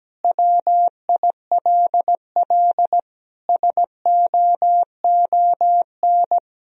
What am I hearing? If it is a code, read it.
Morse code, heard as a keyed tone, about 17 wpm, 700 Hz. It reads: WILL SOON
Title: Sentences from Top 1000 Words - Review Part 3 - 17wpm